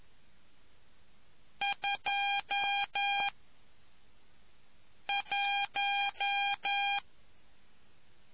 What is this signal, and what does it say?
Morse code, heard as a keyed tone, about 11 wpm, 800 Hz. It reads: 2 1